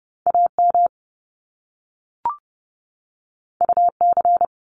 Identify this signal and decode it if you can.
Morse code, heard as a keyed tone, about 30 wpm, 700 Hz. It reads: AM UC
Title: Top Two Letter Combinations in Words 30wpm